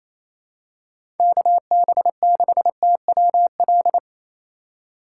Morse code, heard as a keyed tone, about 28 wpm, 700 Hz. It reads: KB6TWL
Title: USA Call Signs - 6 Characters - 3x Character Spacing - 28wpm